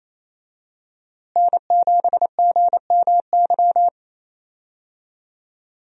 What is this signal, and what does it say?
Morse code, heard as a keyed tone, about 28 wpm, 700 Hz. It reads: N7GMY
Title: USA Call Signs - 5 Characters - Repeated 3x - 28wpm